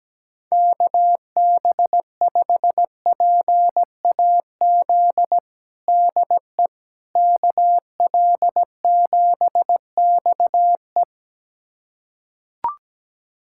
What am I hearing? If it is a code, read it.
Morse code, heard as a keyed tone, about 17 wpm, 700 Hz. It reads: KB5PAZ DE KL7XE
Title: USA Callsign DE Callsign 17wpm